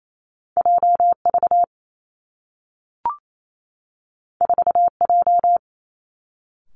Morse code, heard as a keyed tone, about 28 wpm, 700 Hz. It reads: J V 4 J
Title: Two Challenging Letter-Number Combinations 28wpm